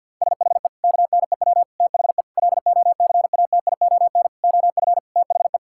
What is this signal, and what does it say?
Morse code, heard as a keyed tone, about 52 wpm, 700 Hz. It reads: SHE KNEW THE LOCATION OF THE
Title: Sentences from Top 1000 Words - Review Part 6 - 50wpm